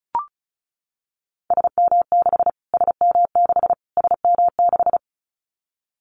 Morse code, heard as a keyed tone, about 34 wpm, 700 Hz. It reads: SM6 SM6 SM6
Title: Callsign Prefixes for Top 25 DXCC - Repeated 3x - 35wpm